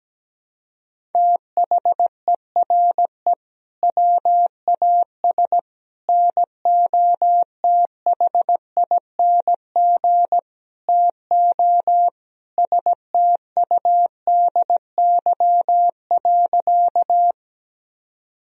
Morse code, heard as a keyed tone, about 17 wpm, 700 Hz. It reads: THERE WAS NOTHING TO STUDY.
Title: Sentences from Top 1000 Words - Review Part 1 - 17wpm